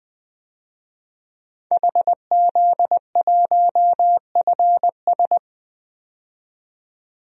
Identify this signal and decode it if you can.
Morse code, heard as a keyed tone, about 20 wpm, 700 Hz. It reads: HZ1FS
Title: DXCC-100 Call Signs - 5 Characters - 20wpm